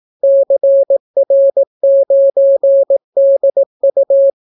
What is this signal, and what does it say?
Morse code, heard as a keyed tone, about 18 wpm, 550 Hz. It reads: CR9DU